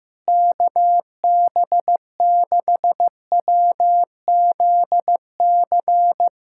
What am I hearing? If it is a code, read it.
Morse code, heard as a keyed tone, about 16 wpm, 700 Hz. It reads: KB6WZC
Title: USA Callsign DE Callsign - 6 Characters - 15wpm